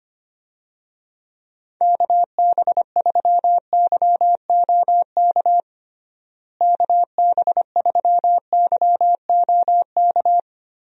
Morse code, heard as a keyed tone, about 25 wpm, 700 Hz. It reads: KB3YOK KB3YOK
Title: USA Call Signs - 6 Characters - Repeated 3x - 25wpm